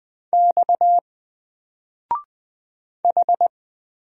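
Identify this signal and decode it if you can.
Morse code, heard as a keyed tone, about 20 wpm, 700 Hz. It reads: X H
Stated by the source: Single letter and number Morse drill at 20 wpm, from N4LCD.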